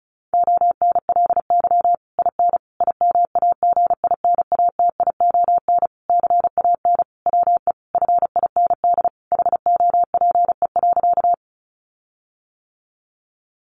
Morse code, heard as a keyed tone, about 34 wpm, 700 Hz. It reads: ONLY IN IMAGINATION CAN WE FIND HOPE.
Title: Sentences from Top 1300 Words 35wpm